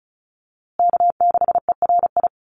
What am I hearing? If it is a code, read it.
Morse code, heard as a keyed tone, about 34 wpm, 700 Hz. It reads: K6ERI
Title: USA Call Signs Repeated 3x 35wpm